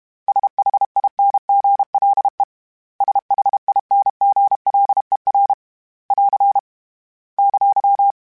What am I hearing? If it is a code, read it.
Morse code, heard as a keyed tone, about 32 wpm, 800 Hz. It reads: SHINGLE SHINGLER <AR> !
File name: Special INGs-mp3s / Special INGs-mp3s / Hidden ings @30